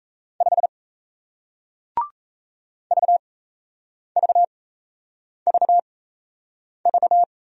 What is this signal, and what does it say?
Morse code, heard as a keyed tone, about 37 wpm, spaced as for 8 wpm, 700 Hz. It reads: 5 VVVV